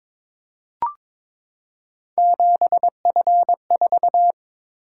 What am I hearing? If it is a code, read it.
Morse code, heard as a keyed tone, about 22 wpm, 700 Hz. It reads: 7F4
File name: practice / 22wpm / Three Letter-Number Combinations - 22wpm